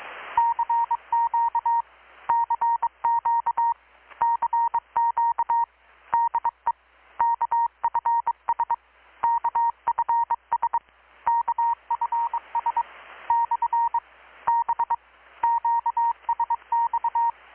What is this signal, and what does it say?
Morse code, heard as a keyed tone, about 22 wpm, 930 Hz. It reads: CQ CQ CQ DE KFS KFS KFS / B QSX
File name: blog / swl / 6-abril-2019-12695-USB-KPS